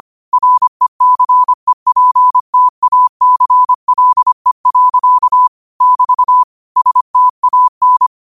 Morse code, heard as a keyed tone, about 25 wpm, 1000 Hz. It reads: RECEPTACLE. <BT> STAN
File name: qotdfiles / 25wpm